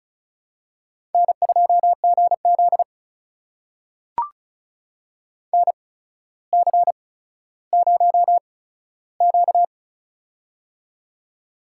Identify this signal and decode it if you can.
Morse code, heard as a keyed tone, about 36 wpm, spaced as for 15 wpm, 700 Hz. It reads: N2GZ NC0Q
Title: USA Call Signs - 4 Characters - 8x Character Spacing - 35wpm